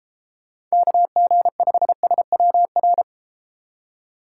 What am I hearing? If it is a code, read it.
Morse code, heard as a keyed tone, about 32 wpm, 700 Hz. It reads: KG5SWR